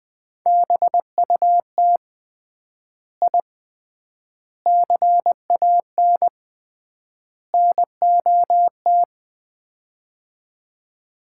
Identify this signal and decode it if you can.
Morse code, heard as a keyed tone, about 20 wpm, 700 Hz. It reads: BUT I CAN NOT
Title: 3x Word-Spacing - Sets of 4 Words - 20wpm